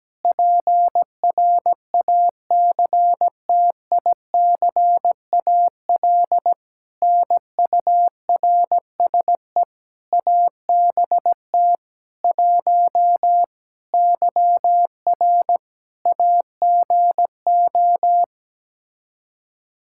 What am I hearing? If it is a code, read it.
Morse code, heard as a keyed tone, about 17 wpm, 700 Hz. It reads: PRACTICAL NURSE ABT 1 YR AGO
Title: Practice QSO - Exchange 3 - More Info - 17wpm